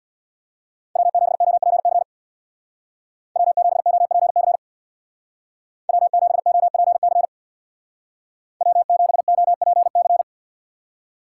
Morse code, heard as a keyed tone, about 60 wpm, 700 Hz. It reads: W7OPC W7OPC W7OPC W7OPC
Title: Speed Racing - USA 1x3 Call Signs 50wpm